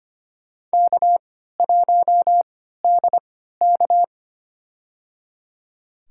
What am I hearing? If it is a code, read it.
Morse code, heard as a keyed tone, about 25 wpm, spaced as for 14 wpm, 700 Hz. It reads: K1DK
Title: USA Call Signs - 3x Character Spacing - 25wpm